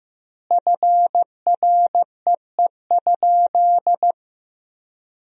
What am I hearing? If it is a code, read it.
Morse code, heard as a keyed tone, about 15 wpm, 700 Hz. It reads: FREE?
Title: 3x Word-Spacing - Sentences from Top 500 Words - 15wpm